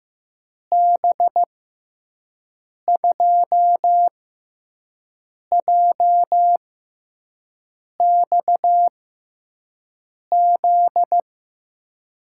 Morse code, heard as a keyed tone, about 15 wpm, spaced as for 5 wpm, 700 Hz. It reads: B2JXZ
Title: USA Call Signs - 6x Character Spacing - 15wpm